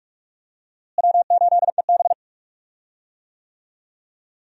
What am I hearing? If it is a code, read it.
Morse code, heard as a keyed tone, about 46 wpm, 700 Hz. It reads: W8EB
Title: USA Call Signs - 4 Characters - Repeated 2x - 45wpm